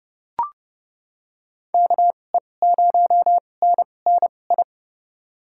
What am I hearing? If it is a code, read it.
Morse code, heard as a keyed tone, about 30 wpm, spaced as for 22 wpm, 700 Hz. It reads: KE0NNI
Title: USA Call Signs - 6 Characters - 2x Character Spacing - 30wpm